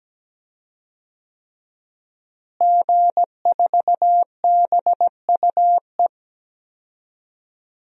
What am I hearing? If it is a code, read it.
Morse code, heard as a keyed tone, about 17 wpm, 700 Hz. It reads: G4BUE
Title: DXCC-100 Call Signs - 5 Characters - Repeated 3x - 17wpm